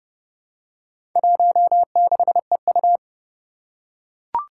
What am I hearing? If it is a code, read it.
Morse code, heard as a keyed tone, about 30 wpm, 700 Hz. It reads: 16EU
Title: Four Challenging Letter-Number Combinations 30wpm